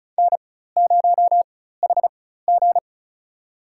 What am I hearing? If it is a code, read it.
Morse code, heard as a keyed tone, about 34 wpm, spaced as for 16 wpm, 700 Hz. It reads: N0HG
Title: USA Call Signs - 4 Characters - 4x Character Spacing - 35wpm